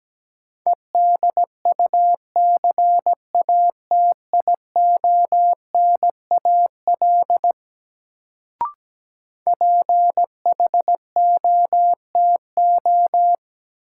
Morse code, heard as a keyed tone, about 17 wpm, 700 Hz. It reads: EDUCATIONAL PHOTO
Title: Sentences from Top 1200 Words - New Word Drill Alt - 17wpm